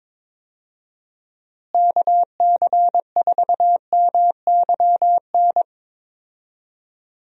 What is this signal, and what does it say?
Morse code, heard as a keyed tone, about 22 wpm, 700 Hz. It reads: KC4MYN
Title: USA Call Signs - 6 Characters - Repeated 2x - 22wpm